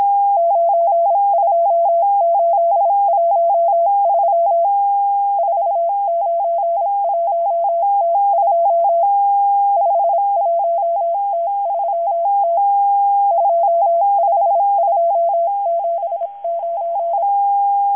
Morse code, heard as a keyed tone, about 26 wpm, 680 Hz. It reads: 92813 491T2 51T3T 25279
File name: HM02- -0437utc-7351khz